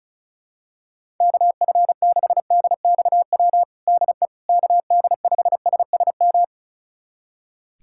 Morse code, heard as a keyed tone, about 36 wpm, 700 Hz. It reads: KF6DXW DE KD5SSM
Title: USA Callsign DE Callsign - 6 Characters - 35wpm